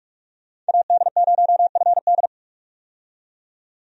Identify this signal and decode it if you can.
Morse code, heard as a keyed tone, about 46 wpm, 700 Hz. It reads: AD0FD
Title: USA Call Signs - 5 Characters - 45wpm